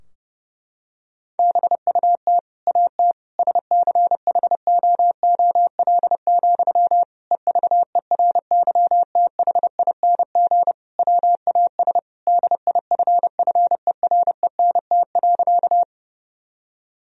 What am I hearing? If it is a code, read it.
Morse code, heard as a keyed tone, about 32 wpm, 700 Hz. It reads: BUT AT SCHOOL, EVERYTHING WAS DIFFERENT.
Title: Sentences from Top 1000 Words - Review Part 3 - 30wpm